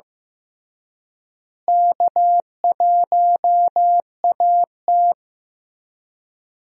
Morse code, heard as a keyed tone, about 15 wpm, 700 Hz. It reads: K1AT
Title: USA Call Signs - 4 Characters - Repeated 3x - 15wpm